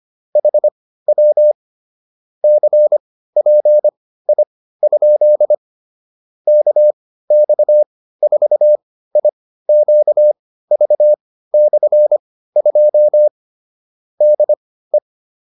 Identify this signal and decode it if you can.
Morse code, heard as a keyed tone, about 25 wpm, spaced as for 15 wpm, 600 Hz. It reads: HW CPI? KX4IQV/2 DE